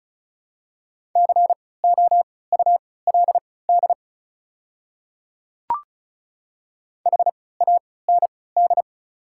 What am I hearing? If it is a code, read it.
Morse code, heard as a keyed tone, about 34 wpm, spaced as for 20 wpm, 700 Hz. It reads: COULD HAND
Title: 3x Character Spacing - Top 200 Words - 35wpm